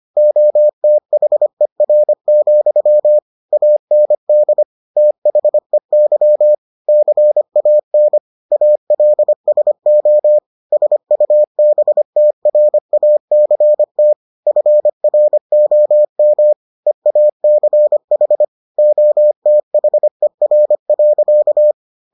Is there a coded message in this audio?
Yes: Morse code from 0.2 to 21.7 s, heard as a keyed tone, about 25 wpm, 600 Hz. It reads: OTHER, AND THEY CAN ALSO SUBTRACT FROM EACH OTHER.